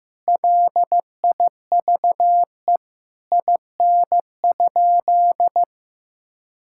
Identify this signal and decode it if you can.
Morse code, heard as a keyed tone, about 15 wpm, 700 Hz. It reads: LIVE IN?
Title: Sentences from Top 1000 Words - Review Part 5 - 15wpm